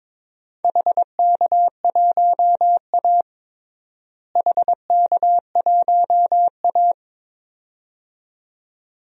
Morse code, heard as a keyed tone, about 22 wpm, 700 Hz. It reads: HK1A HK1A
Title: DXCC-100 Call Signs - 4 Characters - Repeated 2x - 22wpm